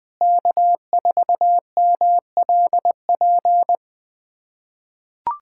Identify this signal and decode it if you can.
Morse code, heard as a keyed tone, about 20 wpm, 700 Hz. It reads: K4MLP
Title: USA Callsign DE Callsign - 5 Characters - 20wpm